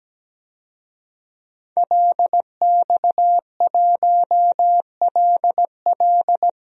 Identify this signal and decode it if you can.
Morse code, heard as a keyed tone, about 17 wpm, 700 Hz. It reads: LX1LL
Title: DXCC-100 Call Signs - 5 Characters - Repeated 2x - 17wpm